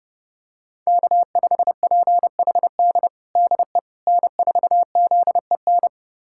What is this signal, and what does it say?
Morse code, heard as a keyed tone, about 30 wpm, 700 Hz. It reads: K5PHD DE N4ZEN
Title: USA Callsign DE Callsign - 5 Characters - 30wpm